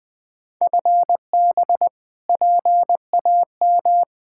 Morse code, heard as a keyed tone, about 20 wpm, 700 Hz. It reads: FB PAM
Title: K1USN SST QSO Practice 20wpm